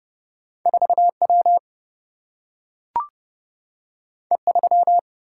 Morse code, heard as a keyed tone, about 30 wpm, 700 Hz. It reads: 4 W E 3
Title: Single-Letter Single-Number 30wpm